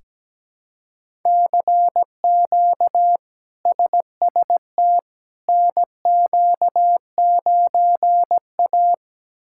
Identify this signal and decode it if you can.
Morse code, heard as a keyed tone, about 17 wpm, 700 Hz. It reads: CQ SST NQ9A
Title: K1USN SST QSO Practice - 17wpm